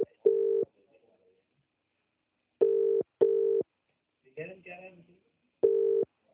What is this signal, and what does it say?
Ringback tone (UK), 400 + 450 Hz, with speech.